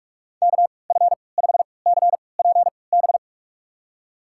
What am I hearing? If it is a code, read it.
Morse code, heard as a keyed tone, about 46 wpm, spaced as for 25 wpm, 700 Hz. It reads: KF5CPB
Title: USA Call Signs - 6 Characters - 3x Character Spacing - 45wpm